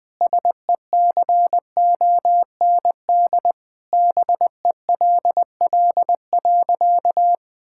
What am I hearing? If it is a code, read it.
Morse code, heard as a keyed tone, about 20 wpm, 700 Hz. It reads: SECOND BELL.
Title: Sentences with 2-Word Phrases - Spoken Beginning - 20wpm